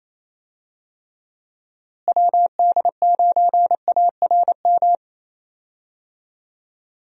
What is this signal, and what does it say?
Morse code, heard as a keyed tone, about 28 wpm, 700 Hz. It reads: WD9ARM
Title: USA Call Signs - 6 Characters - Repeated 2x - 28wpm